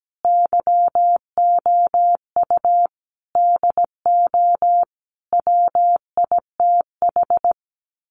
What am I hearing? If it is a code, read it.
Morse code, heard as a keyed tone, about 17 wpm, 700 Hz. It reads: YOU DO WITH
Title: Sets of 5 Words Encore2 17wpm